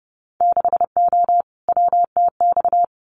Morse code, heard as a keyed tone, about 30 wpm, 700 Hz. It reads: 6O WTX
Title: Winter Field Day Exchanges Encore 30wpm